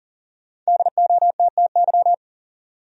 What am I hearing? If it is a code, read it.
Morse code, heard as a keyed tone, about 40 wpm, 700 Hz. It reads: DOTTY
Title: USA Common Names - Max 5 Letters - 40wpm